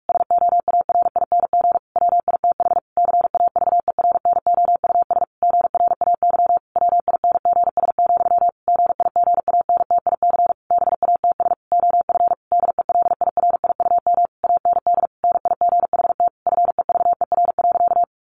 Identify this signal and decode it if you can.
Morse code, heard as a keyed tone, about 44 wpm, 700 Hz. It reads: SOARING WITH CAVERNOUS GRAY WINGS, GIGANTIC BATS OF DELIRIUM AND NIGHT FEVER.
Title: A Sound Of Thunder 45wpm